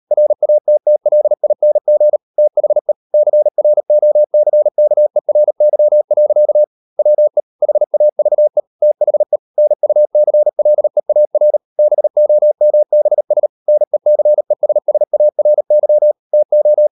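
Morse code, heard as a keyed tone, about 40 wpm, 600 Hz. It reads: RATTLING THE CROCKERY. WE HAVE THE NUCLEAR BOMBS NECESSARY TO
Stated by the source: Morse code practice at 35–39 wpm sending English text of Ken Follett's novel Never, from CW med Gnister.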